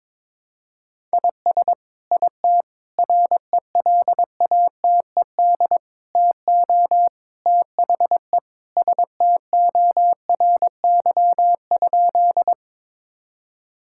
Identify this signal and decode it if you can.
Morse code, heard as a keyed tone, about 22 wpm, 700 Hz. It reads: IS IT RELATED TO THE STORY?
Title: Sentences from Top 1000 Words - Review Part 5 - 22wpm